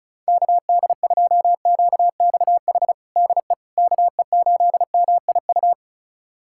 Morse code, heard as a keyed tone, about 36 wpm, 700 Hz. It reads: KD2QXH DE KE8MIU
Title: USA Callsign DE Callsign - 6 Characters - 35wpm